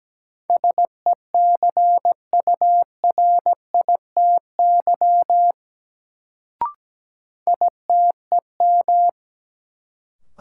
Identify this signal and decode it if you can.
Morse code, heard as a keyed tone, about 17 wpm, 700 Hz. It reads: SECURITY ITEM
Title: Sentences from Top 1400 Words - New Word Drill Alt - 17wpm